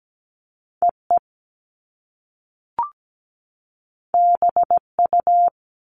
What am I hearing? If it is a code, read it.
Morse code, heard as a keyed tone, about 17 wpm, 700 Hz. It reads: EE BU
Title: Top Two Letter Combinations in Words 17wpm